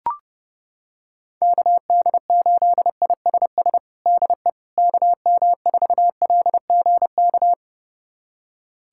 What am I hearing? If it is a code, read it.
Morse code, heard as a keyed tone, about 30 wpm, 690 Hz. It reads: KD8ISS DE KM4LGK
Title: USA Callsign DE Callsign - 6 Characters - 30wpm